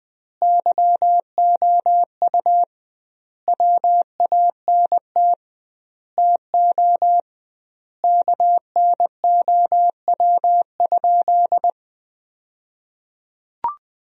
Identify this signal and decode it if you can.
Morse code, heard as a keyed tone, about 20 wpm, 700 Hz. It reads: YOU WANT TO KNOW?
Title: 2x Word-Spacing - Sentences from Top 200 Words - 20wpm